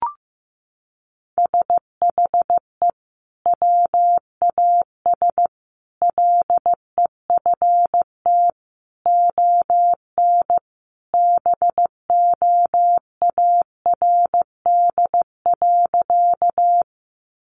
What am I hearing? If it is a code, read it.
Morse code, heard as a keyed tone, about 15 wpm, 700 Hz. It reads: SHE WAS LEFT ON BOARD.
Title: Sentences from Top 700 Words 15wpm